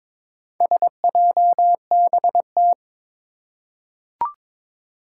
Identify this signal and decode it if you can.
Morse code, heard as a keyed tone, about 22 wpm, 700 Hz. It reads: SJBT
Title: Four Challenging Letter-Number Combinations 22wpm